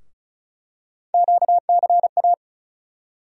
Morse code, heard as a keyed tone, about 36 wpm, 700 Hz. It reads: QCA
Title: USA Call Sign Postfixes - 3 Letters 35wpm